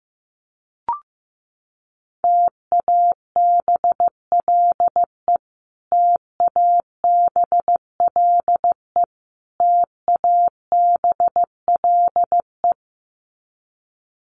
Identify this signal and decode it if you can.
Morse code, heard as a keyed tone, about 15 wpm, 700 Hz. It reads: TABLE TABLE TABLE
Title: Top 500 Words Repeated 15wpm